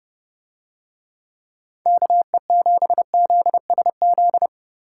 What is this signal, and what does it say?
Morse code, heard as a keyed tone, about 30 wpm, 700 Hz. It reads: KE7ZSZ